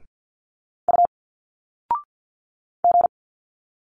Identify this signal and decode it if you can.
Morse code, heard as a keyed tone, about 51 wpm, 700 Hz. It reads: U G
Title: Single Letters 50wpm